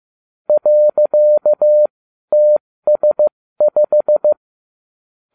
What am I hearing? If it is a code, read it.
Morse code, heard as a keyed tone, about 15 wpm, spaced as for 12 wpm, 600 Hz. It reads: .TS5